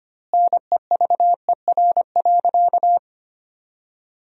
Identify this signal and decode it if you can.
Morse code, heard as a keyed tone, about 25 wpm, 700 Hz. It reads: NEVER.
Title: 2x Word-Spacing - Sentences from Top 400 Words - 25wpm